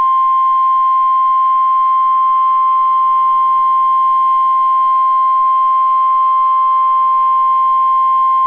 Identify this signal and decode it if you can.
Emergency Alert System (SAME) tones (the 1050 Hz attention tone). It is heard from 0.4 to 7.9 s.